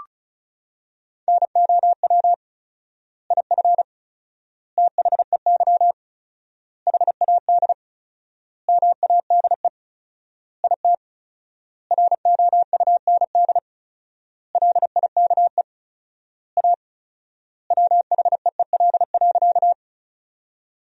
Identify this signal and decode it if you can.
Morse code, heard as a keyed tone, about 36 wpm, 700 Hz. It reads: NOW IF THEY HAD MADE IT ROUND LIKE A WHEEL.
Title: 4x Word-Spacing - Sentences from Top 500 Words - 35wpm